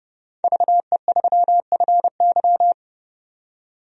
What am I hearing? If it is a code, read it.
Morse code, heard as a keyed tone, about 30 wpm, 700 Hz. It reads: VE3FY
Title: DXCC-100 Call Signs - 5 Characters - 30wpm